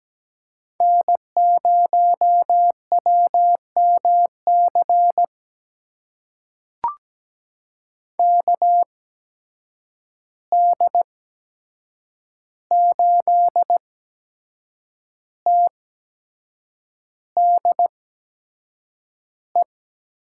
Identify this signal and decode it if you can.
Morse code, heard as a keyed tone, about 17 wpm, spaced as for 7 wpm, 700 Hz. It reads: N0WMC KD8TDE
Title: USA Call Signs - 8x Character Spacing - 17wpm